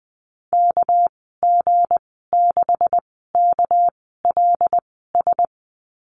Morse code, heard as a keyed tone, about 20 wpm, spaced as for 14 wpm, 700 Hz. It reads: KG6KLS
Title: 238 Lesson 38g - Letter q - Call Signs 20wpm